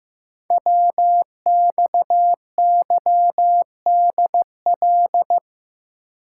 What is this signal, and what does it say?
Morse code, heard as a keyed tone, about 15 wpm, 700 Hz. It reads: WXYDL
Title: Code Groups - 5 Letters - 15wpm